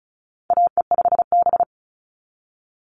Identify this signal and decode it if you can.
Morse code, heard as a keyed tone, about 34 wpm, 700 Hz. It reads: AE5B
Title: NA SOTA Chaser Callsigns 35wpm